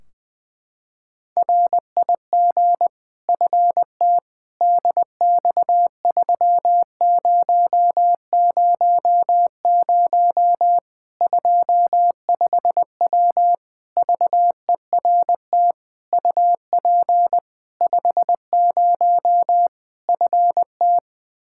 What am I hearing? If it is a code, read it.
Morse code, heard as a keyed tone, about 20 wpm, 700 Hz. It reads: RIG FT DX3000 25W VERT UP 50 FT
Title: Practice QSO - Exchange 2 - Info - 20wpm